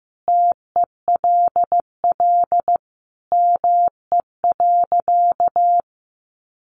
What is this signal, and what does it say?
Morse code, heard as a keyed tone, about 15 wpm, 700 Hz. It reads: TELL ME.
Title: Sentences from Top 200 Words 15wpm